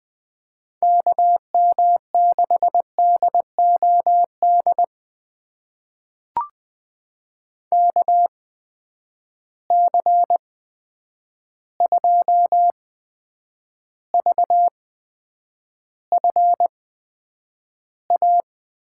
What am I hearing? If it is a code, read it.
Morse code, heard as a keyed tone, about 20 wpm, spaced as for 9 wpm, 700 Hz. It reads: KM6DOD KC2VFA
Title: USA Call Signs - 6 Characters - 8x Character Spacing - 20wpm